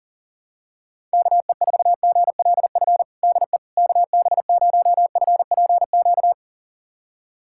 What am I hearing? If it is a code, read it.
Morse code, heard as a keyed tone, about 40 wpm, 700 Hz. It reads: KE4GLF DE KB0FPQ
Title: USA Callsign DE Callsign 40wpm